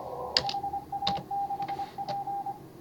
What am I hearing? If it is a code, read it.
Morse code, heard as a keyed tone, about 24 wpm, 810 Hz, with noise. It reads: HA6P
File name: CW / HA5MA RHOB HANG CW 2017